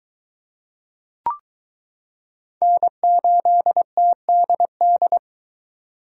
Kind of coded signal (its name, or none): Morse code